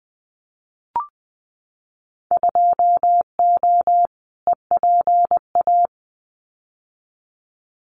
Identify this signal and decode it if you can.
Morse code, heard as a keyed tone, about 20 wpm, 700 Hz. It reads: 2O EPA
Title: Winter Field Day Exchanges Encore 20wpm